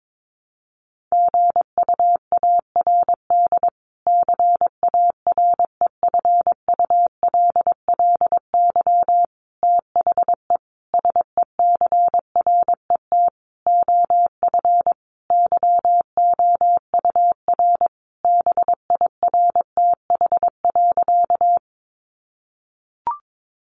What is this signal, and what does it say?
Morse code, heard as a keyed tone, about 22 wpm, 700 Hz. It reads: GUARD CAREFULLY THE SECRET OF YOUR BIRTH.
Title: Sentences from Top 1300 Words 22wpm